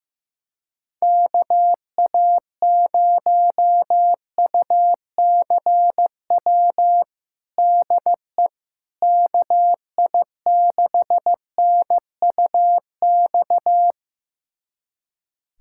Morse code, heard as a keyed tone, about 15 wpm, 700 Hz. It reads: KA0UCW DE KI6NUX
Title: USA Callsign DE Callsign - 6 Characters - 15wpm